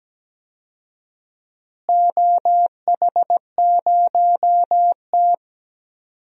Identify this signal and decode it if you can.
Morse code, heard as a keyed tone, about 17 wpm, 700 Hz. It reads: OH0T
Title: DXCC-100 Call Signs - 4 Characters - Repeated 2x - 17wpm